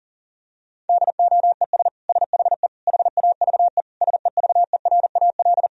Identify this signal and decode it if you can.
Morse code, heard as a keyed tone, about 37 wpm, 700 Hz. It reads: DOES SHE HAVE SEVERAL
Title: Sentences from Top 1000 Words - Review Part 2 - 40wpm